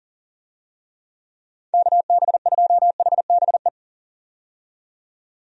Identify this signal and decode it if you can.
Morse code, heard as a keyed tone, about 40 wpm, 700 Hz. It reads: KB2HBE
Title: USA Call Signs - 6 Characters - 40wpm